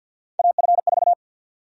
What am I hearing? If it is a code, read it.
Morse code, heard as a keyed tone, about 52 wpm, 700 Hz. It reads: AF4
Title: USA Call Sign Prefixes - Repeated 2x - 50wpm